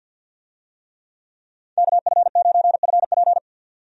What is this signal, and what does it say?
Morse code, heard as a keyed tone, about 51 wpm, 700 Hz. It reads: KF9FP
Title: USA Call Signs - 5 Characters - 50wpm